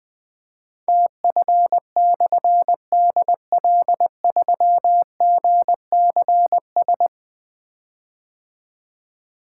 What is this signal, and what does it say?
Morse code, heard as a keyed tone, about 21 wpm, 700 Hz. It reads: TF/DL3GCS
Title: Difficult Call Signs 20wpm